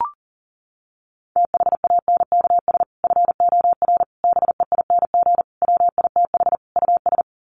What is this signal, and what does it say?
Morse code, heard as a keyed tone, about 37 wpm, 700 Hz. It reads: THANKS FOR BEING WITH US
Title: Sets of 5 Words 40wpm